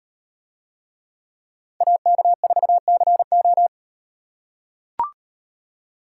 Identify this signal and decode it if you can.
Morse code, heard as a keyed tone, about 35 wpm, 700 Hz. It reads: AK4CO